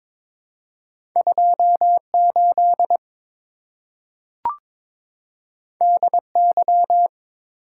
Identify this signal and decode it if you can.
Morse code, heard as a keyed tone, about 22 wpm, 700 Hz. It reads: 28 DY
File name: practice / 22wpm / Two Letter-Number Combinations - 22wpm